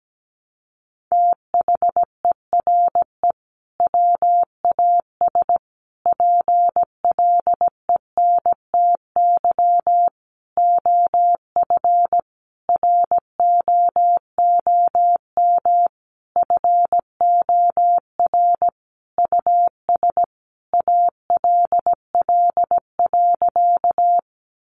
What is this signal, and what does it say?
Morse code, heard as a keyed tone, about 17 wpm, 700 Hz. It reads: THERE WAS PLENTY OF ROOM FOR US ALL.
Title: Sentences from Top 1300 Words 17wpm